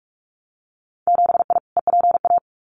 Morse code, heard as a keyed tone, about 46 wpm, 700 Hz. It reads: 7I EPA
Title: Winter Field Day Exchanges Encore 45wpm